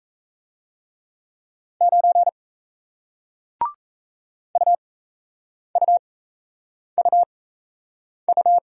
Morse code, heard as a keyed tone, about 39 wpm, spaced as for 8 wpm, 700 Hz. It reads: 9 UUUU